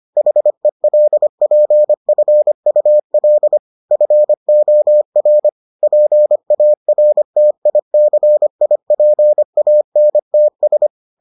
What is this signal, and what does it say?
Morse code, heard as a keyed tone, about 25 wpm, 600 Hz. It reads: HELPFUL FOR PARTICIPANTS